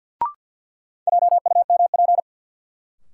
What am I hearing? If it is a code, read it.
Morse code, heard as a keyed tone, about 53 wpm, 700 Hz. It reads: JUMP
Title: Sentences from Top 1300 Words - New Word Drill Alt - 50wpm